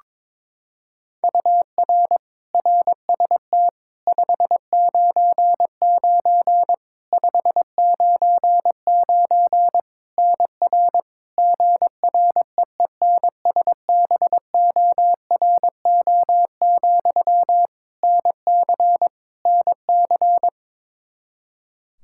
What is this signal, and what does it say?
Morse code, heard as a keyed tone, about 21 wpm, 700 Hz. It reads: UR RST 599 599 NR GREENSBORO, NC NC
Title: Practice QSO - Exchange 1 - Beginning - 22wpm